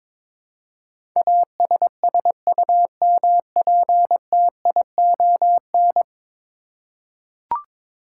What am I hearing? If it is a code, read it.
Morse code, heard as a keyed tone, about 22 wpm, 700 Hz. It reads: ASSUMPTION
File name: practice / 22wpm / 10 Letter Words - First 3 Letters Given - 22wpm